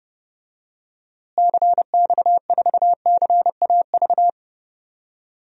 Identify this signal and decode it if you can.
Morse code, heard as a keyed tone, about 30 wpm, 700 Hz. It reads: CX4CAV
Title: DXCC-100 Call Signs - 6 Characters - Repeated 3x - 30wpm